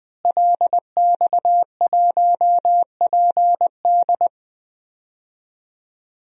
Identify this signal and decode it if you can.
Morse code, heard as a keyed tone, about 20 wpm, 700 Hz. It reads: LX1PD
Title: DXCC-100 Call Signs - 5 Characters - Repeated 3x - 20wpm